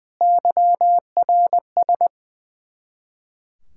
Morse code, heard as a keyed tone, about 21 wpm, 700 Hz. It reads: YRS
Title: Practice QSO - Exchange 3 - More Info - 20wpm